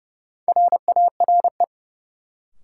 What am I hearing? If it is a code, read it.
Morse code, heard as a keyed tone, about 30 wpm, 700 Hz. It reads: RARE
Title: Sentences from Top 1100 Words - New Word Drill - 30wpm